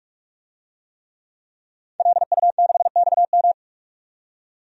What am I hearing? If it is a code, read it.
Morse code, heard as a keyed tone, about 42 wpm, 700 Hz. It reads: LU6XM
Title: DXCC-100 Call Signs - 5 Characters - 45wpm